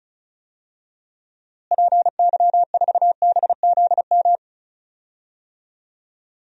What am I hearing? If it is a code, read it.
Morse code, heard as a keyed tone, about 34 wpm, 700 Hz. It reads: PY4BZM
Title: DXCC-100 Call Signs - 6 Characters - Repeated 3x - 35wpm